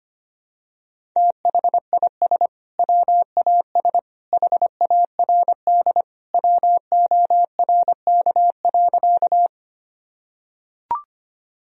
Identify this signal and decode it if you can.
Morse code, heard as a keyed tone, about 25 wpm, 700 Hz. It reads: THIS WAS HARD WORK.
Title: Sentences from Top 1000 Words - Review Part 1 - 25wpm